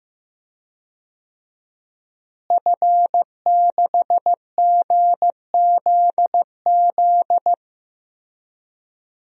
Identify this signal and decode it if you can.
Morse code, heard as a keyed tone, about 15 wpm, 700 Hz. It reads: F6GZZ
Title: DXCC-100 Call Signs - 5 Characters - Repeated 3x - 15wpm